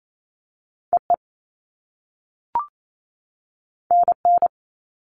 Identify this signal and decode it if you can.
Morse code, heard as a keyed tone, about 28 wpm, 700 Hz. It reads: EE NN
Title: All Cut Numbers - 2-Digit 28wpm